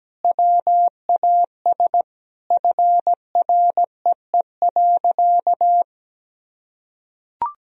Morse code, heard as a keyed tone, about 17 wpm, 700 Hz. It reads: WAS FREE.
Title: Sentences from Top 1000 Words - Review Part 4 - 17wpm